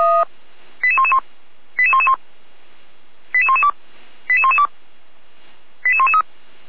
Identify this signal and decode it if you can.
Five-tone selective call (ZVEI), heard as quick runs of five tones, tones of about 70 ms. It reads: 88111 88111 88112 88112 88113